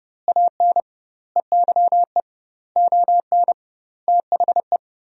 Morse code, heard as a keyed tone, about 32 wpm, 700 Hz. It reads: AN EYE ON THE
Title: 2x Word-Spacing - Sentences from Top 300 Words - 30wpm